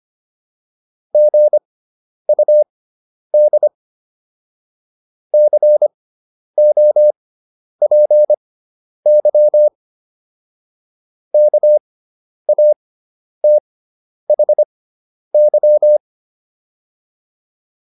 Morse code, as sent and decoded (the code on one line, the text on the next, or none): --. ..- -.. / -.-. --- .--. -.-- / -.- .- - .... -.--
GUD COPY KATHY